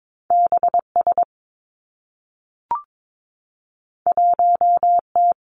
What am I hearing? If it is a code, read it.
Morse code, heard as a keyed tone, about 22 wpm, 700 Hz. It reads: B S 1 T
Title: Two Challenging Letter-Number Combinations 22wpm